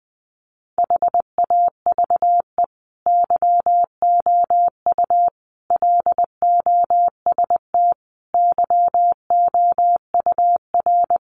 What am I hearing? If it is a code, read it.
Morse code, heard as a keyed tone, about 20 wpm, 700 Hz. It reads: HAVE YOU LOST YOUR
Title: Sentences from Top 1400 Words 20wpm